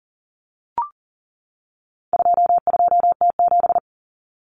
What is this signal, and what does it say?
Morse code, heard as a keyed tone, about 40 wpm, 700 Hz. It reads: 22T7
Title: Common Cut Numbers - 4-Digit 40wpm